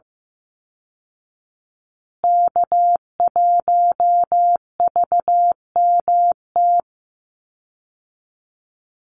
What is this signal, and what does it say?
Morse code, heard as a keyed tone, about 15 wpm, 700 Hz. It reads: K1VMT
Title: Call Signs Repeated 2x - 15wpm